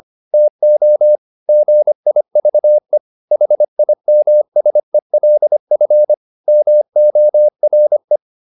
Morse code, heard as a keyed tone, about 25 wpm, 600 Hz. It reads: TO GIVE HIMSELF MORE
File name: the old man and the sea part 1 25wpm